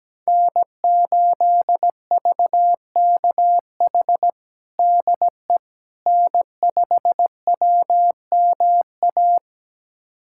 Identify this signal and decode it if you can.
Morse code, heard as a keyed tone, about 17 wpm, 700 Hz. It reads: N8VKH DE N5WMA
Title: USA Callsign DE Callsign - 5 Characters - 17wpm